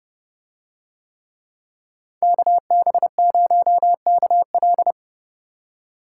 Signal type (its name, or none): Morse code